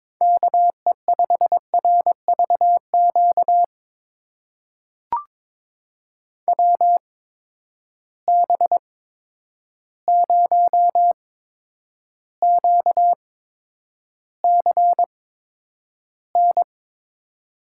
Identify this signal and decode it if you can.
Morse code, heard as a keyed tone, about 22 wpm, spaced as for 9 wpm, 700 Hz. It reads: KE5RVQ WB0QCN
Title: USA Call Signs - 6 Characters - 8x Character Spacing - 22wpm